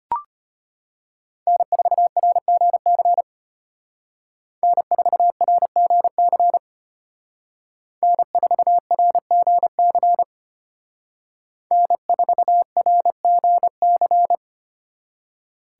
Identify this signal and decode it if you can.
Morse code, heard as a keyed tone, about 32 wpm, 700 Hz. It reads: N4RGC N4RGC N4RGC N4RGC